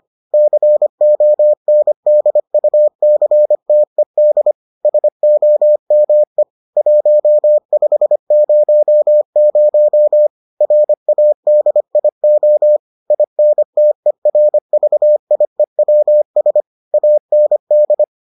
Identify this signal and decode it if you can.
Morse code, heard as a keyed tone, about 25 wpm, 600 Hz. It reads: CONDUCTED SOME 1500 RADIO INTERVIEWS AND